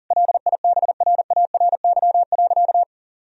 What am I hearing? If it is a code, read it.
Morse code, heard as a keyed tone, about 40 wpm, 700 Hz. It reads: LIBRARY.
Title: Sentences from Top 1000 Words - Review Part 3 - 40wpm